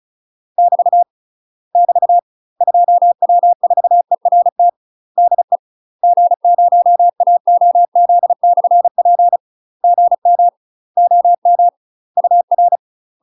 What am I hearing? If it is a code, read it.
Morse code, heard as a keyed tone, about 36 wpm, 700 Hz. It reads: <BT> <BT> 2W4ERT DE G0AOZ/P GM OM UR
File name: UK MASTER QSOs TRAINING / 35WPM / UK MASTER QSOs 0038